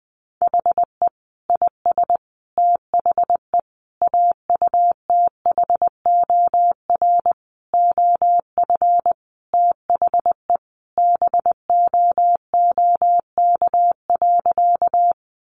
Morse code, heard as a keyed tone, about 20 wpm, 700 Hz. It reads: HE IS THE AUTHOR OF THE BOOK.
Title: Sentences from Top 1200 Words 20wpm